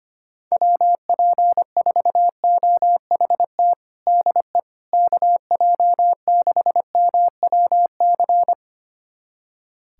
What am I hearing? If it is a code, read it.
Morse code, heard as a keyed tone, about 25 wpm, 700 Hz. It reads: WP4OHT DE KJ6MWC
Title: USA Callsign DE Callsign 25wpm